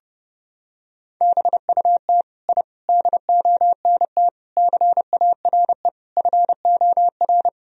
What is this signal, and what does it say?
Morse code, heard as a keyed tone, about 30 wpm, 700 Hz. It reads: BUT I DONT CARE FOR
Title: Sentences from Top 1000 Words - Review Part 1 - 30wpm